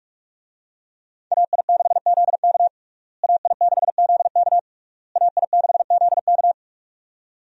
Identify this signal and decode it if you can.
Morse code, heard as a keyed tone, about 46 wpm, 700 Hz. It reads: AI6ZK AI6ZK AI6ZK
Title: USA Call Signs - 5 Characters - Repeated 3x - 45wpm